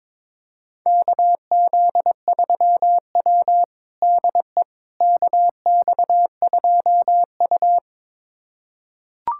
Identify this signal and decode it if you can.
Morse code, heard as a keyed tone, about 22 wpm, 700 Hz. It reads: KZ3W DE KX2U
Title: USA Callsign DE Callsign - 4 Characters - 22wpm